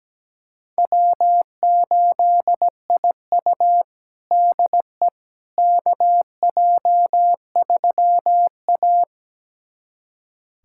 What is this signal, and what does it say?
Morse code, heard as a keyed tone, about 17 wpm, 700 Hz. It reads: W8IU DE KJ3A
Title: USA Callsign DE Callsign - 4 Characters - 17wpm